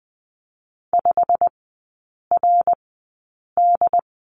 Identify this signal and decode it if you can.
Morse code, heard as a keyed tone, about 20 wpm, spaced as for 8 wpm, 700 Hz. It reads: 5RD
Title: 114 Lesson 19e - Letter w - Three Characters 20wpm